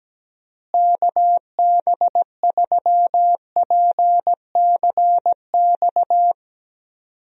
Morse code, heard as a keyed tone, about 17 wpm, 700 Hz. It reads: KB3PCX
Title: USA Call Signs - 6 Characters - Repeated 3x - 17wpm